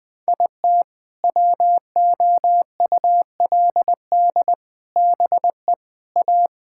Morse code, heard as a keyed tone, about 21 wpm, 700 Hz. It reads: IT WOULD BE A
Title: Sentences with 7-Word Phrases - 20wpm